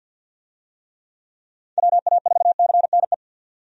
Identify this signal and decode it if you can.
Morse code, heard as a keyed tone, about 46 wpm, 700 Hz. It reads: WA4CNE
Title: USA Call Signs - 6 Characters - 50wpm